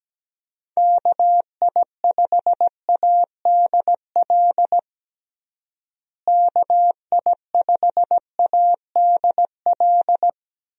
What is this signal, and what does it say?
Morse code, heard as a keyed tone, about 17 wpm, 700 Hz. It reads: KI5ADL KI5ADL
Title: USA Call Signs - 6 Characters - Repeated 3x - 17wpm